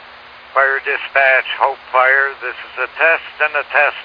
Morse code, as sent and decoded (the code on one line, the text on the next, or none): none